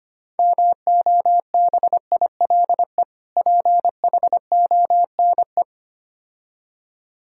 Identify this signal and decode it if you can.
Morse code, heard as a keyed tone, about 25 wpm, 700 Hz. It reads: MOBILE PHONE
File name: practice / 25wpm / Difficult Words - Speed-Racing with Hints - 25wpm